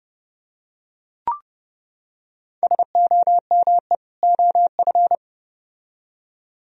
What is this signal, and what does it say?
Morse code, heard as a keyed tone, about 28 wpm, 700 Hz. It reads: SOME OF